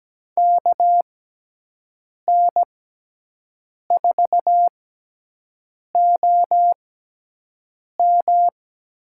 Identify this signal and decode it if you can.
Morse code, heard as a keyed tone, about 17 wpm, spaced as for 6 wpm, 700 Hz. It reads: KN4OM
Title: USA Call Signs - 6 Characters - 6x Character Spacing - 17wpm